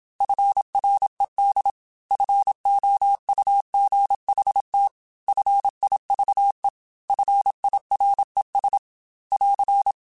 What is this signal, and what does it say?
Morse code, heard as a keyed tone, about 27 wpm, 800 Hz. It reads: FRED FOUGHT FIVE FIRES <AR>